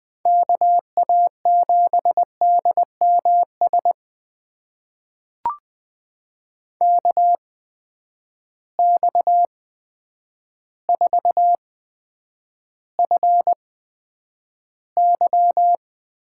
Morse code, heard as a keyed tone, about 20 wpm, spaced as for 9 wpm, 700 Hz. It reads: KA7DMS KX4FY
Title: USA Call Signs - 8x Character Spacing - 20wpm